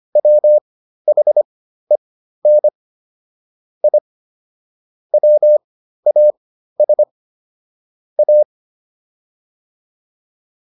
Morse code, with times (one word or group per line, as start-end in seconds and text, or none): 0.2-2.7 WHEN
3.8-4.0 I
5.1-7.0 WAS
8.2-8.4 A